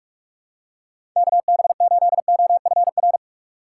Morse code, heard as a keyed tone, about 46 wpm, 700 Hz. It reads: KB8OFR
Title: USA Call Signs - 6 Characters - 2x Character Spacing - 45wpm